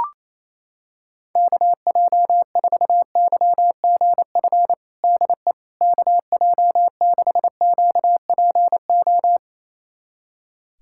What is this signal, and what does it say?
Morse code, heard as a keyed tone, about 27 wpm, 700 Hz. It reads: KJ4YGF DE KJ6QPO
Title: USA Callsign DE Callsign 28wpm